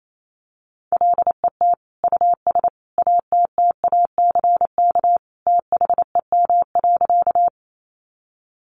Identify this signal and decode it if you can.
Morse code, heard as a keyed tone, about 28 wpm, 700 Hz. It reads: LET US ATTACK THEM.
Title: Sentences from Top 1500 Words 28wpm